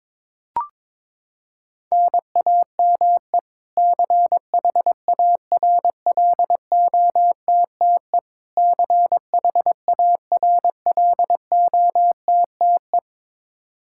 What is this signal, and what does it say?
Morse code, heard as a keyed tone, about 22 wpm, 700 Hz. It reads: NAME CHARLOTTE CHARLOTTE
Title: Practice QSO - Exchange 1 - Beginning - 22wpm